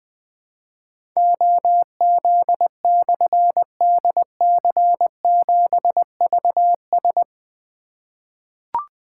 Morse code, heard as a keyed tone, about 20 wpm, 700 Hz. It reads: OZ/DC7VS
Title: Difficult Call Signs 20wpm